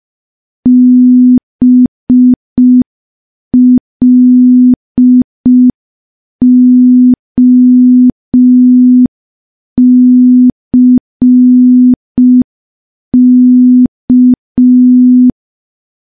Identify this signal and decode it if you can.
Morse code, heard as a keyed tone, about 5 wpm, 250 Hz. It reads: BLOCK
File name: morse / morse-training-05WPM